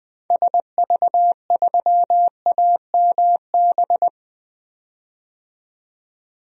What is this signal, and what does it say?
Morse code, heard as a keyed tone, about 20 wpm, 700 Hz. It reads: SV3AMB
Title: DXCC-100 Call Signs - 6 Characters - 20wpm